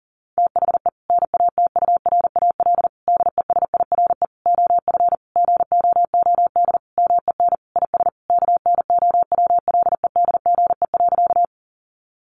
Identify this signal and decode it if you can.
Morse code, heard as a keyed tone, about 40 wpm, 700 Hz. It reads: THE NATURAL DESIRE OF GOOD MEN IS KNOWLEDGE.
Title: Sentences from Top 1100 Words 40wpm